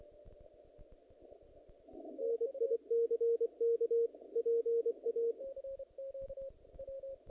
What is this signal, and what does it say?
Morse code, heard as a keyed tone, about 24 wpm, 460 Hz, with radio noise. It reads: NICK PA
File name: contests / naqp-cw-k5tr-op-k5pi- -0100